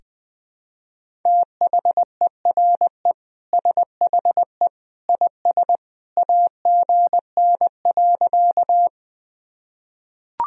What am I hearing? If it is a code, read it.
Morse code, heard as a keyed tone, about 20 wpm, 700 Hz. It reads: THERE SHE IS AGN.
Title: Sentences from Top 200 Words and Abbreviations 20wpm